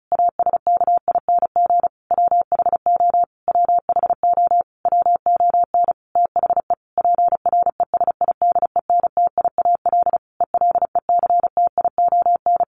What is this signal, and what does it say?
Morse code, heard as a keyed tone, about 34 wpm, 700 Hz. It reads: ASKING WHO WHO WON THE PRESIDENTIAL ELECTION
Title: A Sound Of Thunder 35wpm